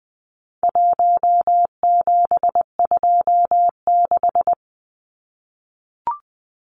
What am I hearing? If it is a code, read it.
Morse code, heard as a keyed tone, about 20 wpm, 700 Hz. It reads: 1726
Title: Numbers - 4-Digit 20wpm